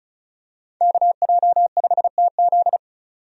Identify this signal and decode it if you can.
Morse code, heard as a keyed tone, about 36 wpm, 700 Hz. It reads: KJ5TZ
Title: USA Call Signs - 5 Characters - 35wpm